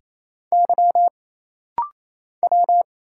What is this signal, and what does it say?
Morse code, heard as a keyed tone, about 27 wpm, 700 Hz. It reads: Y W
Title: Single Letters - Rapid-Fire 28wpm